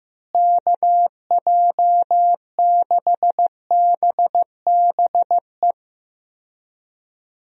Morse code, heard as a keyed tone, about 15 wpm, 700 Hz. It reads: KJ6BBE
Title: USA Call Signs - 6 Characters - Repeated 3x - 15wpm